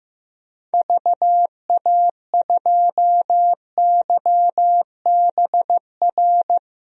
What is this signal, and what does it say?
Morse code, heard as a keyed tone, about 15 wpm, 700 Hz. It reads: VA2YBR
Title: DXCC-100 Call Signs - 6 Characters - 15wpm